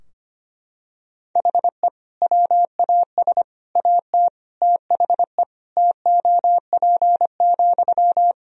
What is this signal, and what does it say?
Morse code, heard as a keyed tone, about 25 wpm, 700 Hz. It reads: HE WAS AT THE TOP,
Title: Sentences from Top 1000 Words - Review Part 2 - 25wpm